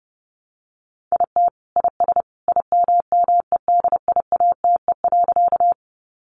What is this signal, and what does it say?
Morse code, heard as a keyed tone, about 28 wpm, 700 Hz. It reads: IT IS IMMEDIATE.
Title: Sentences from Top 1200 Words 30wpm